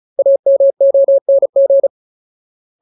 Morse code, heard as a keyed tone, about 34 wpm, 550 Hz. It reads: AMONG
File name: Ultimate Word List / 35WPM Ultimate